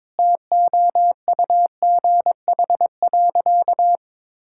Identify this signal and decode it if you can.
Morse code, heard as a keyed tone, about 22 wpm, 700 Hz. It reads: TOUGH.
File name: Sentences with 7-Word Phrases - 22wpm